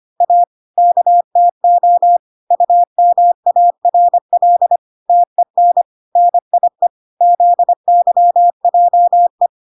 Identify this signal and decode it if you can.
Morse code, heard as a keyed tone, about 25 wpm, 700 Hz. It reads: A KTO UMARL TEN NIE ZYJE